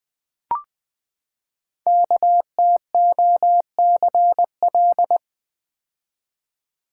Morse code, heard as a keyed tone, about 20 wpm, 700 Hz. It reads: KTOCL
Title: Code Groups - 5 Letters - 20wpm